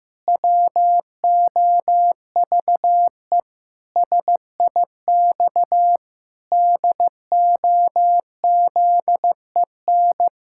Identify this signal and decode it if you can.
Morse code, heard as a keyed tone, about 15 wpm, 700 Hz. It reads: WOVE SIX DOZEN
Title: Pangram Challenge v2 - 15wpm